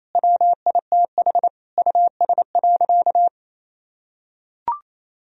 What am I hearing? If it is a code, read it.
Morse code, heard as a keyed tone, about 28 wpm, 700 Hz. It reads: WITH US.
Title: Sentences from Top 1000 Words - Review Part 3 - 28wpm